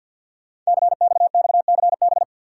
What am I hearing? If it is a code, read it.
Morse code, heard as a keyed tone, about 50 wpm, 690 Hz. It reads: CXXCB
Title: Code Groups - 5 Letters - 50wpm